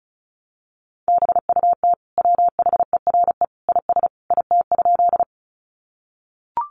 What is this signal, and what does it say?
Morse code, heard as a keyed tone, about 34 wpm, 700 Hz. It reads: BUT WHERE IS IT?
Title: Sentences from Top 200 Words 35wpm